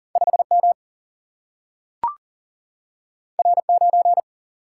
Morse code, heard as a keyed tone, about 40 wpm, 700 Hz. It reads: 5M R9
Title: Single-Letter Single-Number 40wpm